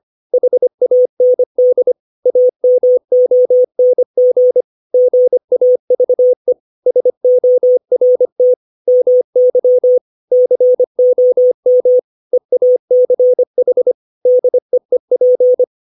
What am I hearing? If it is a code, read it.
Morse code, heard as a keyed tone, about 25 wpm, 500 Hz. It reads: HAND AMONG GAVE SORT MY COM EACH DEEP
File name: 632 words / 632 Words 25WPM